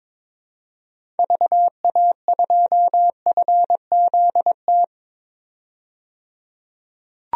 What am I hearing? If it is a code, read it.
Morse code, heard as a keyed tone, about 22 wpm, 700 Hz. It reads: VA2FZT
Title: DXCC-100 Call Signs - 6 Characters - 22wpm